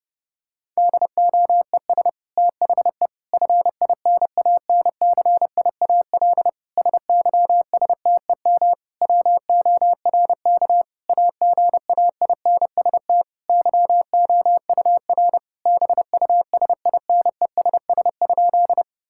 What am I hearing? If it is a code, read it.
Morse code, heard as a keyed tone, about 29 wpm, 700 Hz. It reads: DOES THE FINANCIAL SYSTEM WORK AGAINST YOUR BUSINESS?
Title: Sentences from Top 1000 Words - Review Part 5 - 30wpm